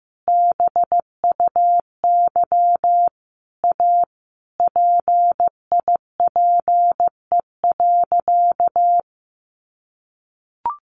Morse code, heard as a keyed tone, about 15 wpm, 700 Hz. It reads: BUY A PIPE.
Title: Sentences from Top 1500 Words 15wpm